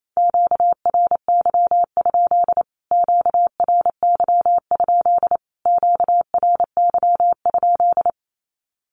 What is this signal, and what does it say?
Morse code, heard as a keyed tone, about 28 wpm, 700 Hz. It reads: QRY? QRY? QRY?
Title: Q-Codes Repeated 3x 28wpm